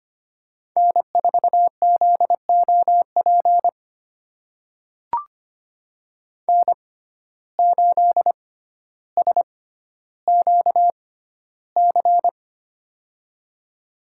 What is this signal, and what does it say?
Morse code, heard as a keyed tone, about 25 wpm, spaced as for 13 wpm, 700 Hz. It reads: N4ZOP N8SQC
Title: USA Call Signs - 5 Characters - 6x Character Spacing - 25wpm